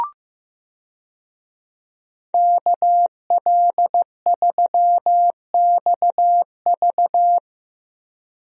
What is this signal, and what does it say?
Morse code, heard as a keyed tone, about 15 wpm, 700 Hz. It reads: KL3XV
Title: USA Call Signs - 5 Characters - Repeated 3x - 15wpm